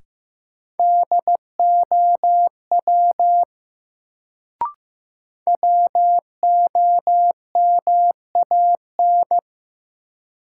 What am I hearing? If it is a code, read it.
Morse code, heard as a keyed tone, about 15 wpm, 700 Hz. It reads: DOW WOMAN
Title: Sets of 5 Words - New Word Drill - 15wpm